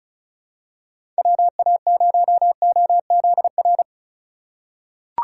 Morse code, heard as a keyed tone, about 36 wpm, 700 Hz. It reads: WA0OZR
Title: USA Call Signs - 6 Characters - 3x Character Spacing - 35wpm